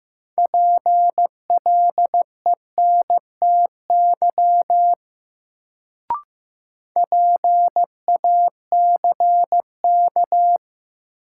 Morse code, heard as a keyed tone, about 15 wpm, 700 Hz. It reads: PLENTY PACK
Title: Sentences from Top 1300 Words - New Word Drill Alt - 15wpm